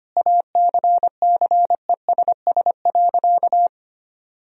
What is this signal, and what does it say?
Morse code, heard as a keyed tone, about 25 wpm, 700 Hz. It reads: ACCESS.
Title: Sentences from Top 1000 Words - Review Part 4 - 25wpm